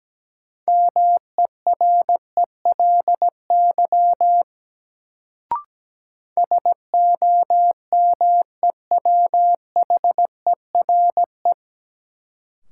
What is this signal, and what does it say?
Morse code, heard as a keyed tone, about 17 wpm, 700 Hz. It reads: MERELY SOMEWHERE
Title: Sentences from Top 1400 Words - New Word Drill Alt - 17wpm